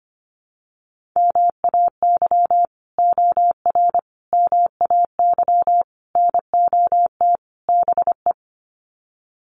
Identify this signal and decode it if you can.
Morse code, heard as a keyed tone, about 25 wpm, 700 Hz. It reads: MAY OR MAY NOT BE
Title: Sets of 5 Words 25wpm